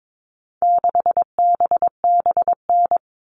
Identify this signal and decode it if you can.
Morse code, heard as a keyed tone, about 22 wpm, 700 Hz. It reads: 6BBN
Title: All Cut Numbers - 4-Digit 22wpm